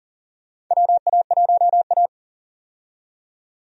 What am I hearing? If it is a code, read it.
Morse code, heard as a keyed tone, about 40 wpm, 700 Hz. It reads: WA1A
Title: USA Call Signs - 4 Characters - 40wpm